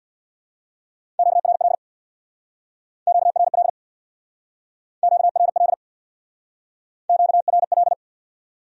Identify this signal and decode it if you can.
Morse code, heard as a keyed tone, about 64 wpm, 700 Hz. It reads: QRL QRL QRL QRL